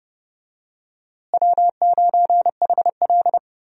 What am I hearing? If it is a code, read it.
Morse code, heard as a keyed tone, about 30 wpm, 700 Hz. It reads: W9HL
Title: USA Call Signs - 4 Characters - Repeated 2x - 30wpm